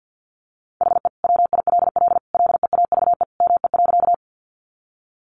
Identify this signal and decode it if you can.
Morse code, heard as a keyed tone, about 50 wpm, 700 Hz. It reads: HE WILL LEAVE ME.
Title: Sentences from Top 400 Words 50wpm